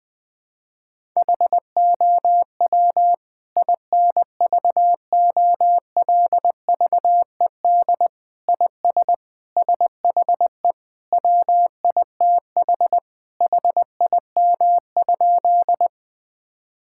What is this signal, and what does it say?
Morse code, heard as a keyed tone, about 20 wpm, 700 Hz. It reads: HOW INVOLVED IS SHE WITH HIM?
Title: Sentences from Top 1000 Words - Review Part 5 - 20wpm